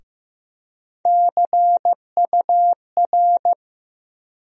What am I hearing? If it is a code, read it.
Morse code, heard as a keyed tone, about 15 wpm, 700 Hz. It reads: CUR
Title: USA Call Sign Postfixes - 3 Letters 15wpm